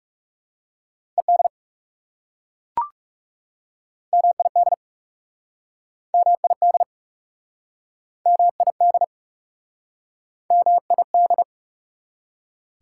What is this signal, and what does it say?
Morse code, heard as a keyed tone, about 38 wpm, 700 Hz. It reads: ED MID MID MID MID